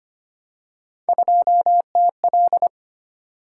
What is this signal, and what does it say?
Morse code, heard as a keyed tone, about 25 wpm, 700 Hz. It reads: 2TL
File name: Three Letter-Number Combinations - 25wpm